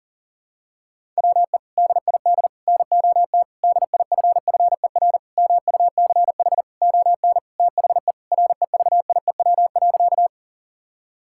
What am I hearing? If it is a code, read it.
Morse code, heard as a keyed tone, about 40 wpm, 700 Hz. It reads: WE DID NOT DIFFER MUCH ON THE REVIEW.
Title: Sentences from Top 1000 Words - Review Part 6 - 40wpm